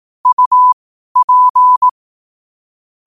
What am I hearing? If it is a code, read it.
Morse code, heard as a keyed tone, about 18 wpm, spaced as for 12 wpm, 1000 Hz. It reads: UP
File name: qotdfiles / 12wpm